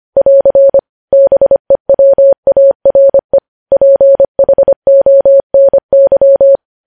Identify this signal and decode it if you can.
Morse code, heard as a keyed tone, about 25 wpm, 550 Hz. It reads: <AR> BEWARE PHONY
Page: news-headlines-in-morse-code-at-25-wpm